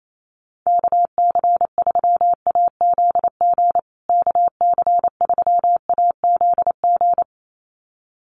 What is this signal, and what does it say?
Morse code, heard as a keyed tone, about 28 wpm, 700 Hz. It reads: KC3AZG KC3AZG
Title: USA Call Signs Repeated 2x 28wpm